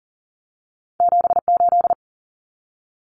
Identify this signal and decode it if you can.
Morse code, heard as a keyed tone, about 40 wpm, 700 Hz. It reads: 78
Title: Common Cut Numbers - 2-Digit 40wpm